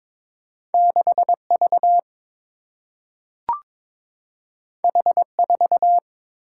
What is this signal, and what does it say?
Morse code, heard as a keyed tone, about 22 wpm, 700 Hz. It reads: 6 V H 4
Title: Single-Letter Single-Number 22wpm